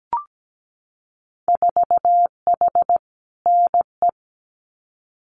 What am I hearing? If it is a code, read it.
Morse code, heard as a keyed tone, about 17 wpm, 700 Hz. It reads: 4H NE
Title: Winter Field Day Exchanges Encore 17wpm